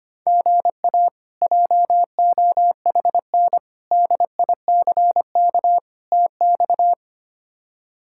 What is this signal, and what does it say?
Morse code, heard as a keyed tone, about 25 wpm, 700 Hz. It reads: GA JOHN DICK TX
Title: K1USN SST QSO Practice - 25wpm